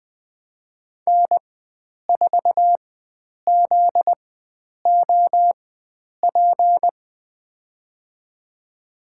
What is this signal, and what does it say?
Morse code, heard as a keyed tone, about 20 wpm, spaced as for 9 wpm, 700 Hz. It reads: N4ZOP
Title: USA Call Signs - 5 Characters - 4x Character Spacing - 20wpm